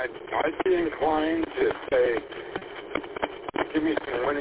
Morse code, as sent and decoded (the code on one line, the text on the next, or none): none